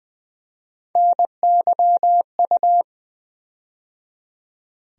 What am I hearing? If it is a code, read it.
Morse code, heard as a keyed tone, about 20 wpm, 700 Hz. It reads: NYU
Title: USA Call Sign Postfixes - 3 Letters 20wpm